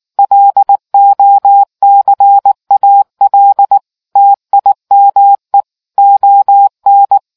Morse code, heard as a keyed tone, about 19 wpm, 800 Hz. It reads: LOCAL TIME ON